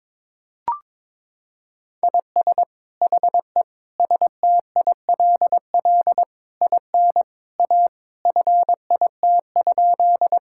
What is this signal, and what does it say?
Morse code, heard as a keyed tone, about 22 wpm, 700 Hz. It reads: IS HE STILL IN A FIT?
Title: Sentences from Top 1000 Words - Review Part 5 - 22wpm